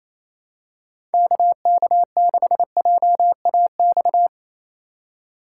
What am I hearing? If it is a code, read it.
Morse code, heard as a keyed tone, about 27 wpm, 700 Hz. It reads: KK6JAX
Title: USA Call Signs - 6x Character Spacing - 28wpm